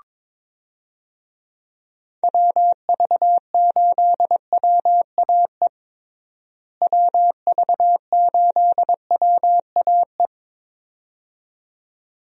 Morse code, heard as a keyed tone, about 23 wpm, 700 Hz. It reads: WV8WAE WV8WAE
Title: USA Call Signs - 6 Characters - Repeated 2x - 22wpm